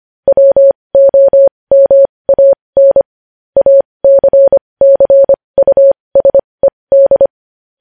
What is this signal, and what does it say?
Morse code, heard as a keyed tone, about 25 wpm, spaced as for 20 wpm, 550 Hz. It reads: WOMAN ACCUSED
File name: podcast / News headlines-1x-25x20